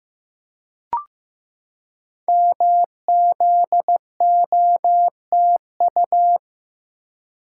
Morse code, heard as a keyed tone, about 15 wpm, 700 Hz. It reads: MZOTU
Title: Code Groups - 5 Letters - 15wpm